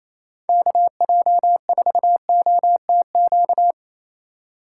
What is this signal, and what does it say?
Morse code, heard as a keyed tone, about 28 wpm, 700 Hz. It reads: KJ4OTQ
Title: USA Call Signs - 6 Characters - 4x Character Spacing - 28wpm